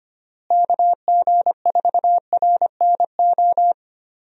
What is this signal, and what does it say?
Morse code, heard as a keyed tone, about 25 wpm, 700 Hz. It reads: KG4RNO
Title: USA Call Signs - 6 Characters - Repeated 3x - 25wpm